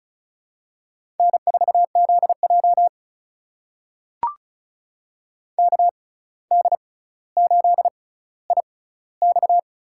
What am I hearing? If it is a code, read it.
Morse code, heard as a keyed tone, about 36 wpm, spaced as for 17 wpm, 700 Hz. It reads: N4ZJ KD8IX
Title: USA Call Signs - 6x Character Spacing - 35wpm